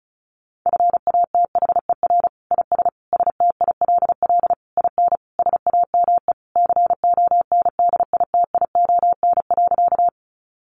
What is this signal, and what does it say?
Morse code, heard as a keyed tone, about 34 wpm, 700 Hz. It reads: FATHER IS STILL IN SAME CONDITION.
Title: Sentences from Top 1000 Words 35wpm